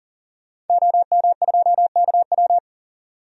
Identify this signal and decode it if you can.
Morse code, heard as a keyed tone, about 40 wpm, 700 Hz. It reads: OM2KW
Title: DXCC-100 Call Signs - 5 Characters - 40wpm